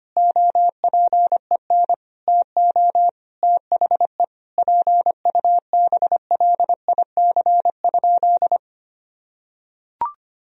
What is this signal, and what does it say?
Morse code, heard as a keyed tone, about 25 wpm, 700 Hz. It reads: OPEN TO THE PUBLIC?
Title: Sentences from Top 1000 Words - Review Part 3 - 25wpm